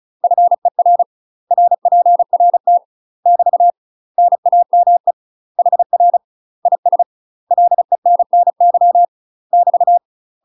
Morse code, heard as a keyed tone, about 36 wpm, 700 Hz. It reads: FER RPRT <BT> NAME HR IS LENNY <BT>
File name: UK MASTER QSOs TRAINING / 35WPM / UK MASTER QSOs 0041